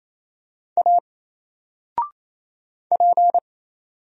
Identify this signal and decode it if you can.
Morse code, heard as a keyed tone, about 27 wpm, 700 Hz. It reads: A P
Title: Single Letter-Number Alt 28wpm